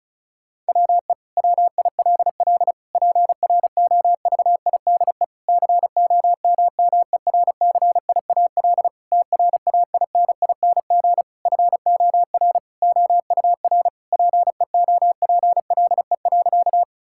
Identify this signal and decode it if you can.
Morse code, heard as a keyed tone, about 34 wpm, 700 Hz. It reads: WE WILL PROVIDE COMMERCIAL TRAINING FOR OUR PEOPLE.
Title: Sentences from Top 1000 Words - Review Part 5 - 35wpm